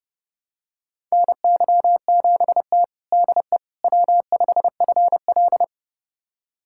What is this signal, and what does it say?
Morse code, heard as a keyed tone, about 30 wpm, 700 Hz. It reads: NY7T DE W5FL
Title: USA Callsign DE Callsign - 4 Characters - 30wpm